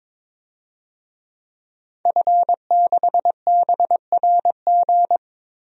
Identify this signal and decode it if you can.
Morse code, heard as a keyed tone, about 22 wpm, 700 Hz. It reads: F6BRG
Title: DXCC-100 Call Signs - 5 Characters - Repeated 2x - 22wpm